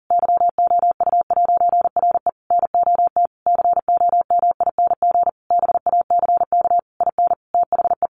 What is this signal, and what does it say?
Morse code, heard as a keyed tone, about 40 wpm, 700 Hz. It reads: YOU'RE NOT COMING BACK IN THE
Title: A Sound Of Thunder 40wpm